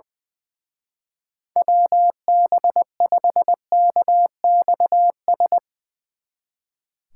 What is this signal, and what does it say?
Morse code, heard as a keyed tone, about 20 wpm, 700 Hz. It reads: WB5KXS
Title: Speed Racing - USA Call Signs 20wpm